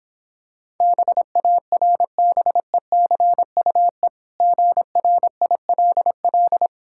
Morse code, heard as a keyed tone, about 26 wpm, 700 Hz. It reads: BARBECUE GRILL